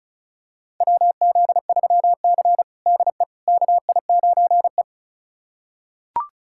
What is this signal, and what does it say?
Morse code, heard as a keyed tone, about 34 wpm, 700 Hz. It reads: WZ3C DE KI9E
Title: USA Callsign DE Callsign - 4 Characters - 35wpm